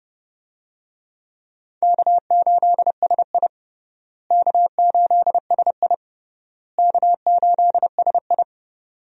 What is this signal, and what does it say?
Morse code, heard as a keyed tone, about 30 wpm, 700 Hz. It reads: K8SI K8SI K8SI
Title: USA Call Signs - 4 Characters - Repeated 3x - 30wpm